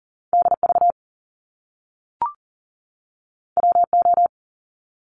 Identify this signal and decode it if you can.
Morse code, heard as a keyed tone, about 40 wpm, 700 Hz. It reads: DV WO
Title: Top Two Letter Combinations in Words 40wpm